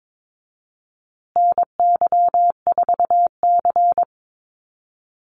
Morse code, heard as a keyed tone, about 22 wpm, 700 Hz. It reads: NY4C
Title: USA Call Signs Repeated 3x 22wpm